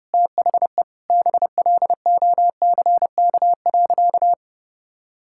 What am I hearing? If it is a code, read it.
Morse code, heard as a keyed tone, about 32 wpm, 700 Hz. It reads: THE BLOCK.
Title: Sentences from Top 1600 Words 30wpm